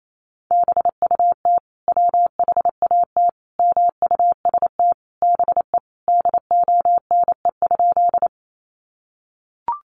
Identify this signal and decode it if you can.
Morse code, heard as a keyed tone, about 28 wpm, 700 Hz. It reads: BUT WHAT MUST BE DONE?
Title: Sentences from Top 500 Words 28wpm